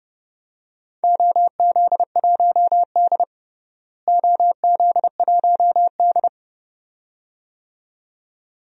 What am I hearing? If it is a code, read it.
Morse code, heard as a keyed tone, about 31 wpm, 700 Hz. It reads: OZ1D OZ1D
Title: DXCC-100 Call Signs - 4 Characters - Repeated 2x - 30wpm